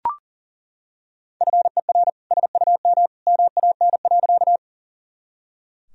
Morse code, heard as a keyed tone, about 40 wpm, 700 Hz. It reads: FER SUM MAN.
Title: Sentences from Top 200 Words and Abbreviations 40wpm